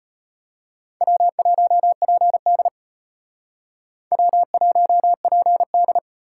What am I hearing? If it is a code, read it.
Morse code, heard as a keyed tone, about 32 wpm, 700 Hz. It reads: W1PD W1PD